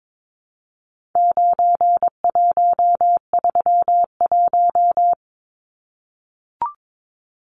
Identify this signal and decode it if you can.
Morse code, heard as a keyed tone, about 22 wpm, 700 Hz. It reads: 9131
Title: Numbers - 4-Digit 22wpm